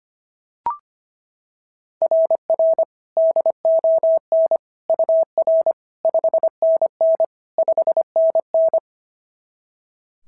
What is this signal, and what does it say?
Morse code, heard as a keyed tone, about 25 wpm, 690 Hz. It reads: RR DON UR 5NN 5NN
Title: SOTA QSO Practice 25wpm